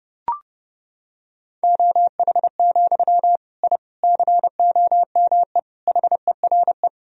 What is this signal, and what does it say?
Morse code, heard as a keyed tone, about 30 wpm, 700 Hz. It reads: OH, I COME HERE
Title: Sentences from Top 1000 Words - Review Part 2 - 30wpm